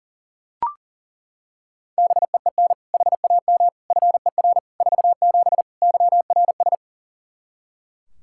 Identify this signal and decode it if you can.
Morse code, heard as a keyed tone, about 40 wpm, 700 Hz. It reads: BEEN HAM FER 47 YRS
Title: Practice QSO - Exchange 3 - More Info - 40wpm